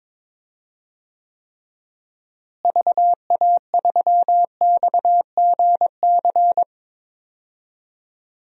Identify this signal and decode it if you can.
Morse code, heard as a keyed tone, about 22 wpm, 700 Hz. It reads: VA3XGC
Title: DXCC-100 Call Signs - 6 Characters - Repeated 3x - 22wpm